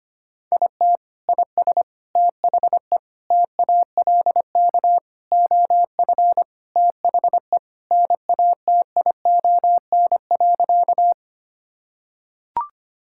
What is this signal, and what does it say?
Morse code, heard as a keyed tone, about 25 wpm, 700 Hz. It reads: IT IS THE TALK OF THE NATION.
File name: practice / 25wpm / Sentences with 4-Word Phrases - 25wpm